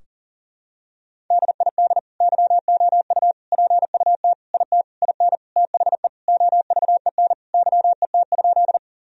Morse code, heard as a keyed tone, about 40 wpm, 700 Hz. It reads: DID YOU PUT IT IN THE OVEN YET?
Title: Sentences from Top 1000 Words - Review Part 4 - 40wpm